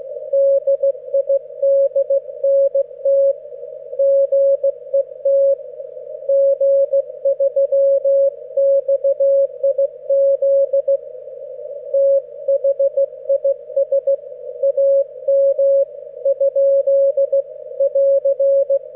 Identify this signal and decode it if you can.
Morse code, heard as a keyed tone, about 16 wpm, 550 Hz. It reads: DIDNT GET G3XIZ THIS AM ? <AR>
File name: GW3UEP / EI0CF 30W